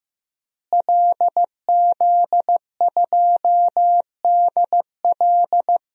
Morse code, heard as a keyed tone, about 15 wpm, 700 Hz. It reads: LZ2DL
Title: DXCC-100 Call Signs - 5 Characters - Repeated 3x - 15wpm